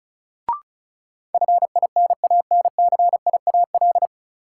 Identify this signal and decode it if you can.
Morse code, heard as a keyed tone, about 34 wpm, 700 Hz. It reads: FINANCIAL
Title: Sentences from Top 1000 Words - Review Part 5 - New Word Drill - 35wpm